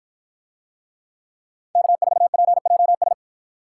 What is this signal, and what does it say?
Morse code, heard as a keyed tone, about 53 wpm, 700 Hz. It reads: K4PJS